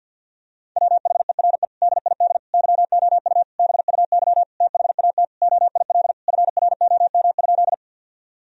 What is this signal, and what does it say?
Morse code, heard as a keyed tone, about 50 wpm, 700 Hz. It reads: WHERE DID YOU BUY THAT OIL FROM?
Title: Sentences from Top 1000 Words - Review Part 4 - 50wpm